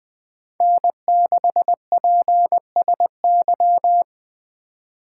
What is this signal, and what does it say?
Morse code, heard as a keyed tone, about 21 wpm, 700 Hz. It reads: N6PSY
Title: Speed Racing - USA 1x3 Call Signs 17wpm